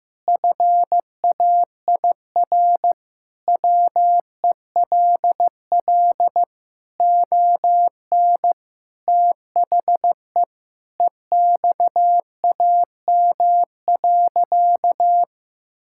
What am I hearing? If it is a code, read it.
Morse code, heard as a keyed tone, about 15 wpm, 700 Hz. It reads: FAIR WELL ON THE EXAM.
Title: Sentences from Top 1000 Words - Review Part 5 - 15wpm